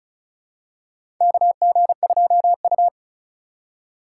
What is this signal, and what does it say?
Morse code, heard as a keyed tone, about 36 wpm, 700 Hz. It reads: KG2U
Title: USA Call Signs - 4 Characters - 4x Character Spacing - 35wpm